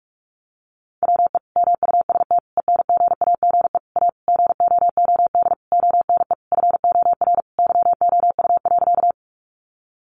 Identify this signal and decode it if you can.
Morse code, heard as a keyed tone, about 44 wpm, 700 Hz. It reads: WE MUST ENGAGE A GOOD ONE FOR YOU.
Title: Sentences from Top 1500 Words 45wpm